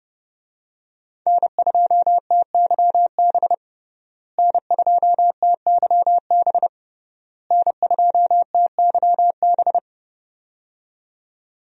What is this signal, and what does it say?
Morse code, heard as a keyed tone, about 30 wpm, 700 Hz. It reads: N2TYB N2TYB N2TYB
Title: USA Call Signs - 5 Characters - Repeated 3x - 30wpm